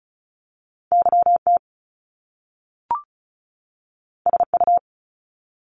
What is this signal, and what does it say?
Morse code, heard as a keyed tone, about 36 wpm, 700 Hz. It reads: YT SU
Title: Top Two Letter Combinations in Words 35wpm